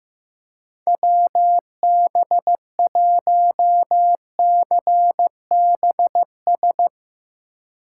Morse code, heard as a keyed tone, about 15 wpm, 700 Hz. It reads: WB1CBS
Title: Speed Racing - USA Call Signs 15wpm